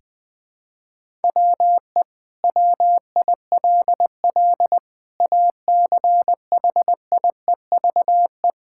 Morse code, heard as a keyed tone, about 21 wpm, 700 Hz. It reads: WE WILL ACHIEVE
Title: Sentences from Top 1000 Words - Review Part 6 - 20wpm